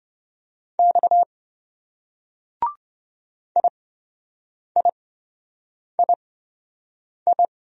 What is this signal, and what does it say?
Morse code, heard as a keyed tone, about 26 wpm, spaced as for 7 wpm, 700 Hz. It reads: X IIII